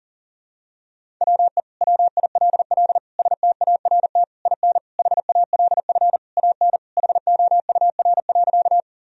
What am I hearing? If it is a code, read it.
Morse code, heard as a keyed tone, about 40 wpm, 700 Hz. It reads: WE WILL START IN HALF AN HOUR.
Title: Sentences from Top 1000 Words - Review Part 1 - 40wpm